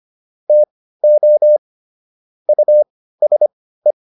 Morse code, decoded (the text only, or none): TO USE